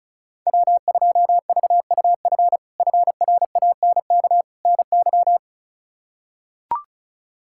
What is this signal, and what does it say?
Morse code, heard as a keyed tone, about 36 wpm, 700 Hz. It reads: W2VUF FRANK NY
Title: K1USN SST QSO Practice - 35wpm